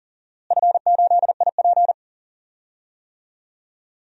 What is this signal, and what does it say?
Morse code, heard as a keyed tone, about 40 wpm, 690 Hz. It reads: F8IP
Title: DXCC-100 Call Signs - 4 Characters - Repeated 2x - 40wpm